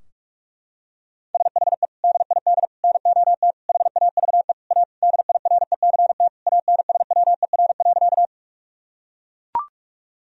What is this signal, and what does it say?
Morse code, heard as a keyed tone, about 42 wpm, 700 Hz. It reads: SHE DID NOT HAVE A DIRECT ANSWER.
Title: Sentences from Top 1000 Words - Review Part 2 - 45wpm